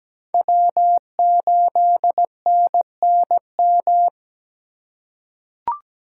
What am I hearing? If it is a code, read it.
Morse code, heard as a keyed tone, about 17 wpm, 700 Hz. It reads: W8NNM
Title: USA Call Signs - 5 Characters - 4x Character Spacing - 17wpm